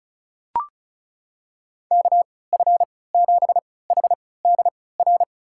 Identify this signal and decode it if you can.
Morse code, heard as a keyed tone, about 36 wpm, spaced as for 20 wpm, 700 Hz. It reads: KF7HDR
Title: USA Call Signs - 6 Characters - 3x Character Spacing - 35wpm